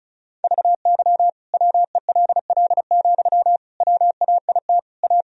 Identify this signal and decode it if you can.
Morse code, heard as a keyed tone, about 36 wpm, 700 Hz. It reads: VY WELL, WAIT A
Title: Sentences from Top 500 Words and Abbreviations 35wpm